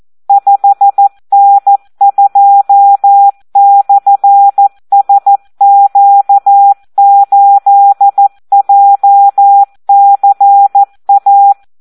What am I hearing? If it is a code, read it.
Morse code, heard as a keyed tone, about 14 wpm, 800 Hz. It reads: 5N2/SQ8JCA
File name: Audio practica CW / SD 1 14